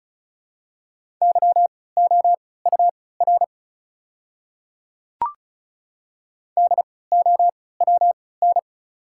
Morse code, heard as a keyed tone, about 36 wpm, spaced as for 20 wpm, 690 Hz. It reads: YOUR DOWN
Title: 3x Character Spacing - Top 100 Words - 35wpm